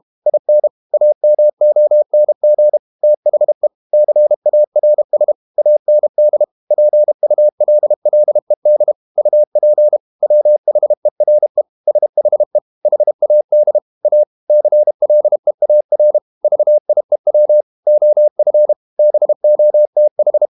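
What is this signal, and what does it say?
Morse code, heard as a keyed tone, about 32 wpm, 600 Hz. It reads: IN AMONG THE CARS AND PULLED UP WHERE SHE HAD A CLEAR VIEW OF BOTH
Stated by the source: Morse code practice at 32 wpm from CW med Gnister, reading Ken Follett's novel Never in English.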